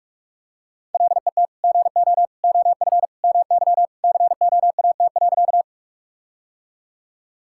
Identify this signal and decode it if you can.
Morse code, heard as a keyed tone, about 46 wpm, 700 Hz. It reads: LET GO OF MY COAT.
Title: Sentences from Top 1600 Words 45wpm